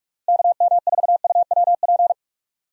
Morse code, heard as a keyed tone, about 41 wpm, 700 Hz. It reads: KM4UWP
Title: USA Call Signs - 6 Characters - Repeated 3x - 45wpm